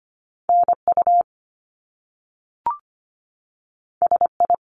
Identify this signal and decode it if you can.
Morse code, heard as a keyed tone, about 25 wpm, 700 Hz. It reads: NU SI
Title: Top Two Letter Combinations in Words 25wpm